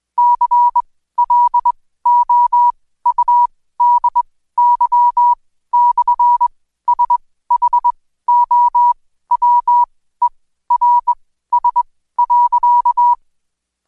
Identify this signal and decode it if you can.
Morse code, heard as a keyed tone, about 21 wpm, spaced as for 14 wpm, 970 Hz. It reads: CLOUDY/SHOWERS.
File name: Code Quick / AudioDownld2 / Seg124-Day29-DayDrill 29